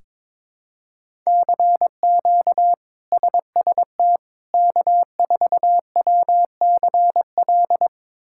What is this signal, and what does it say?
Morse code, heard as a keyed tone, about 22 wpm, 700 Hz. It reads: CQ SST K4WCL
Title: K1USN SST QSO Practice - 22wpm